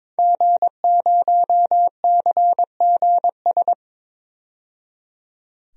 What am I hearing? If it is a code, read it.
Morse code, heard as a keyed tone, about 22 wpm, 700 Hz. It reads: G0CGS
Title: DXCC-100 Call Signs - 5 Characters - 22wpm